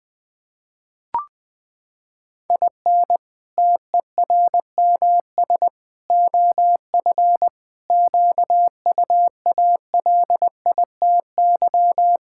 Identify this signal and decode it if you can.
Morse code, heard as a keyed tone, about 20 wpm, 700 Hz. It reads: IN TERMS OF QUALITY
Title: Sentences with 3-Word Phrases - 20wpm